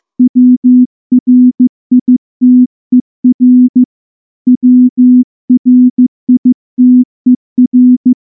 Morse code, heard as a keyed tone, about 14 wpm, 250 Hz. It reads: WRITER WRITER
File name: morse-training-15WPM